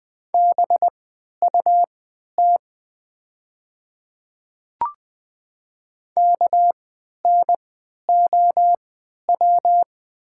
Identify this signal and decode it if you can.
Morse code, heard as a keyed tone, about 20 wpm, spaced as for 11 wpm, 700 Hz. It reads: BUT KNOW
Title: 3x Character Spacing - Top 300 Words - 20wpm